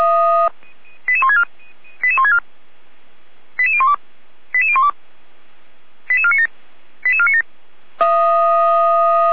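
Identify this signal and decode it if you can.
Five-tone selective call (ZVEI), heard as quick runs of five tones, tones of about 70 ms. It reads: 88164 88164 88012 88012 88497 88497
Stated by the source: ZVEI five-tone call received by SDR, archived by fire brigade Feuerwehr Orlamünde.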